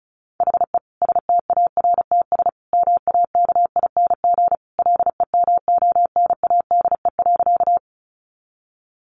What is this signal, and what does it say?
Morse code, heard as a keyed tone, about 34 wpm, 700 Hz. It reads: HE STARTS MAKING LEMONADE.
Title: Additional Easy Phrases 35wpm